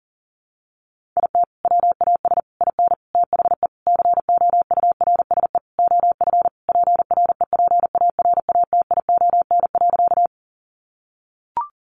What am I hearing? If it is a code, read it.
Morse code, heard as a keyed tone, about 40 wpm, 700 Hz. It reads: IT WAS IN THE COURSE OF PREPARATION.
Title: Sentences from Top 1400 Words 40wpm